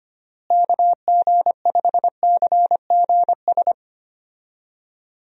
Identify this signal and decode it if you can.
Morse code, heard as a keyed tone, about 25 wpm, 700 Hz. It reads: KG5CGS
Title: USA Call Signs - 6 Characters - Repeated 3x - 25wpm